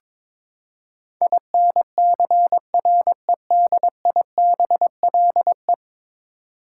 Morse code, heard as a keyed tone, about 22 wpm, 700 Hz. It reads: INCREDIBLE
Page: morse-code-ninja-practice-22wpm